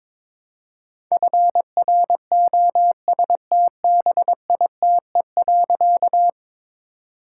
Morse code, heard as a keyed tone, about 22 wpm, 700 Hz. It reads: FROSTBITE.
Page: morse-code-ninja-practice-22wpm